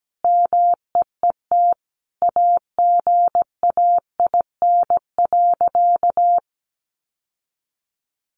Morse code, heard as a keyed tone, about 17 wpm, 700 Hz. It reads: MEET AGAIN.
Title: Sentences from Top 1100 Words 17wpm